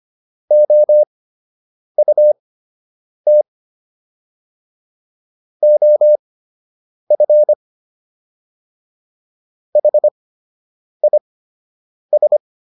Morse code, heard as a keyed tone, about 25 wpm, spaced as for 8 wpm, 600 Hz. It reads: OUT OF HIS